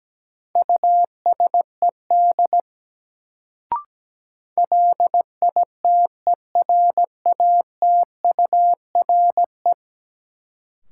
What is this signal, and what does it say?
Morse code, heard as a keyed tone, about 17 wpm, 700 Hz. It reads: USED LITERATURE
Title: Sentences from Top 600 Words - New Word Drill - 17wpm